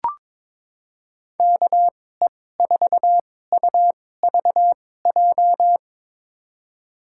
Morse code, heard as a keyed tone, about 22 wpm, spaced as for 16 wpm, 700 Hz. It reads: KE4UVJ
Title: USA Call Signs - 6 Characters - 2x Character Spacing - 22wpm